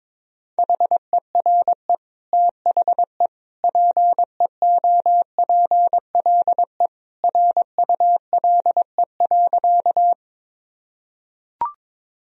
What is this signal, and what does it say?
Morse code, heard as a keyed tone, about 22 wpm, 700 Hz. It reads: HERE THE PEOPLE RULE.
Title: Sentences from Top 1000 Words - Review Part 2 - 22wpm